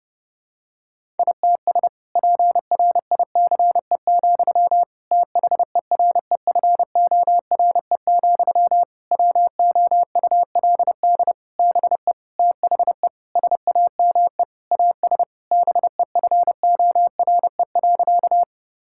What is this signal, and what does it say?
Morse code, heard as a keyed tone, about 30 wpm, 700 Hz. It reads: ITS PRICE, THEREFORE, WOULD BE THE SAME AS BEFORE.
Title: Sentences from Top 1000 Words - Review Part 5 - 30wpm